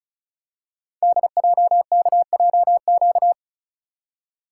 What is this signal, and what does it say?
Morse code, heard as a keyed tone, about 36 wpm, 700 Hz. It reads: DJKJQ
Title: Code Groups - 5 Letters - 35wpm